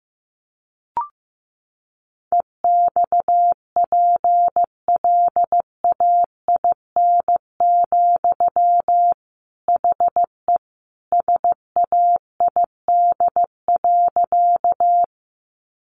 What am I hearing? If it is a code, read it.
Morse code, heard as a keyed tone, about 15 wpm, 700 Hz. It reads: EXPLAIN, HE SAID.
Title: Sentences from Top 1100 Words 15wpm